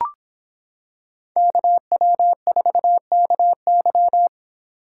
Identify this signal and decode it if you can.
Morse code, heard as a keyed tone, about 26 wpm, 700 Hz. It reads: KW4KY